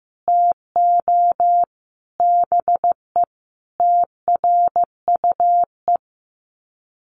Morse code, heard as a keyed tone, about 15 wpm, 700 Hz. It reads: TO BE TRUE
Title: Sets of 5 Words Encore 15wpm